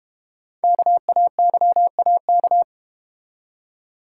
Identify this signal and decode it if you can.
Morse code, heard as a keyed tone, about 32 wpm, 700 Hz. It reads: KAYAK